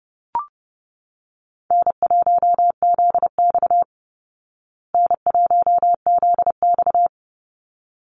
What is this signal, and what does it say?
Morse code, heard as a keyed tone, about 30 wpm, 700 Hz. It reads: N1ZX N1ZX
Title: Easy Call Signs Repeated 2x - 30wpm